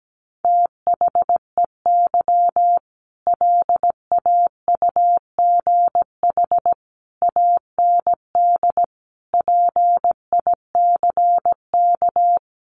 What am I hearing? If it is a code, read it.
Morse code, heard as a keyed tone, about 17 wpm, 700 Hz. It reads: THEY LAUGH AND PICK
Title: Sentences from Top 1000 Words 17wpm